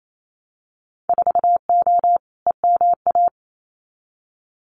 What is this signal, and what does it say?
Morse code, heard as a keyed tone, about 28 wpm, 700 Hz. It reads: 4O EMA
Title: Winter Field Day Exchanges 28wpm